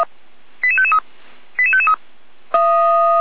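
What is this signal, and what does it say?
Five-tone selective call (ZVEI), heard as quick runs of five tones, tones of about 70 ms. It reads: 88553 88553
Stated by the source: Received by SDR as ZVEI five-tone selective call.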